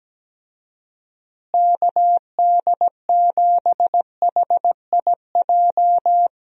Morse code, heard as a keyed tone, about 17 wpm, 700 Hz. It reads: KD7HIJ
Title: USA Call Signs - 6 Characters - Repeated 2x - 17wpm